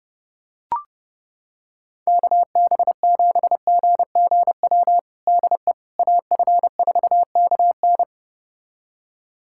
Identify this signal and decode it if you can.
Morse code, heard as a keyed tone, about 30 wpm, 700 Hz. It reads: KB7GGW DE AF4KN
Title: USA Callsign DE Callsign 30wpm